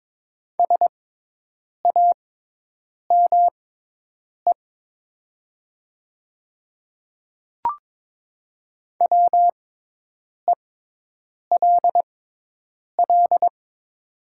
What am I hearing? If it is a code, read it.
Morse code, heard as a keyed tone, about 22 wpm, spaced as for 8 wpm, 700 Hz. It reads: SAME WELL
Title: 6x Character Spacing - Top 300 Words - 22wpm